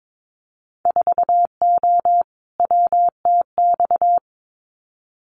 Morse code, heard as a keyed tone, about 22 wpm, 700 Hz. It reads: 4O WTX
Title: Winter Field Day Exchanges Encore 22wpm